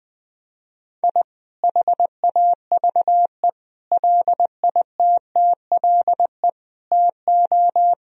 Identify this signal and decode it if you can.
Morse code, heard as a keyed tone, about 20 wpm, 700 Hz. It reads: I HAVE LITTLE TO
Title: Sentences from Top 1000 Words - Review Part 1 - 20wpm